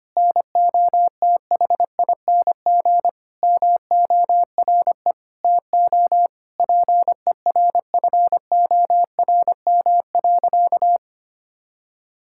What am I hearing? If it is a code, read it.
Morse code, heard as a keyed tone, about 25 wpm, 700 Hz. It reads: NOTHING MORE TO PERFORM.
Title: Sentences from Top 1000 Words - Review Part 5 - 25wpm